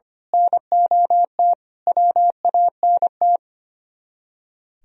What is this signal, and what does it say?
Morse code, heard as a keyed tone, about 28 wpm, 700 Hz. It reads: NOT WANT
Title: Speed-Racing - Sets of 5 Words 25wpm